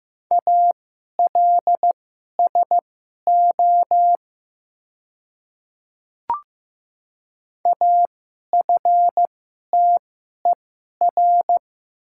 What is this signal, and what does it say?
Morse code, heard as a keyed tone, about 15 wpm, spaced as for 11 wpm, 700 Hz. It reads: ALSO AFTER
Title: 2x Character Spacing - Top 300 Words - 15wpm